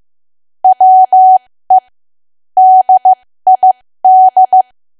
Morse code, heard as a keyed tone, about 15 wpm, spaced as for 13 wpm, 750 Hz. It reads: WE DID
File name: cwa / W1AW-Level2 / 13WPM